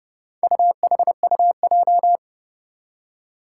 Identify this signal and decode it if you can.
Morse code, heard as a keyed tone, about 30 wpm, 700 Hz. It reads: UHUJ
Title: Four Challenging Letter-Number Combinations 30wpm